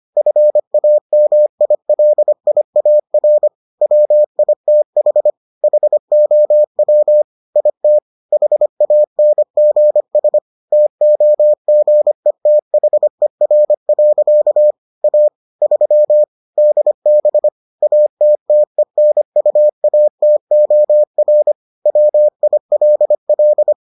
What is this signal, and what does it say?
Morse code, heard as a keyed tone, about 25 wpm, 600 Hz. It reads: FAMILIAR WITH HOW IT HANGS TOGETHER. A 3 DB ATTENUATOR WILL